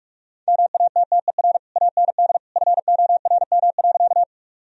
Morse code, heard as a keyed tone, about 44 wpm, 700 Hz. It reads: MATTER AND FORM.
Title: Sentences from Top 1000 Words - Review Part 1 - 45wpm